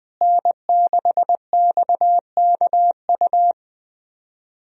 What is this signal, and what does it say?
Morse code, heard as a keyed tone, about 21 wpm, 700 Hz. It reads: N6XKU
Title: USA Call Signs - 5 Characters - Repeated 3x - 20wpm